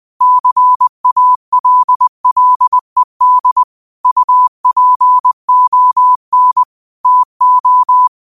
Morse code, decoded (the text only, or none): CALLED UPON TO